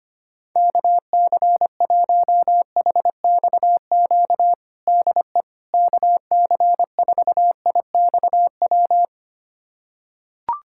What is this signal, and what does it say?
Morse code, heard as a keyed tone, about 25 wpm, 700 Hz. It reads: KC1HXQ DE KC4IXW
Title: USA Callsign DE Callsign - 6 Characters - 25wpm